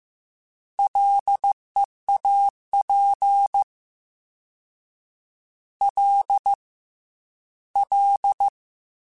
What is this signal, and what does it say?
Morse code, heard as a keyed tone, about 15 wpm, 800 Hz. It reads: LEAP L L
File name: Combinations / Combinations-mp3s / Combinations@13